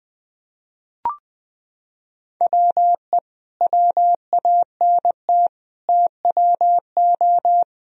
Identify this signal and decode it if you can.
Morse code, heard as a keyed tone, about 20 wpm, 700 Hz. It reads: WE WANT TWO
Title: Sentences from Top 1000 Words - Review Part 1 - 20wpm